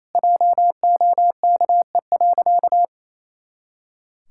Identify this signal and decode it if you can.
Morse code, heard as a keyed tone, about 28 wpm, 700 Hz. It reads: JOKE.
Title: Pangram Challenge v2 - 28wpm